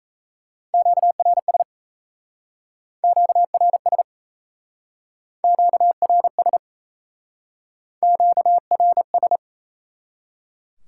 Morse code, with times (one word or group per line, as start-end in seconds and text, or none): 0.7-1.6 QRS
3.0-4.0 QRS
5.4-6.6 QRS
8.0-9.4 QRS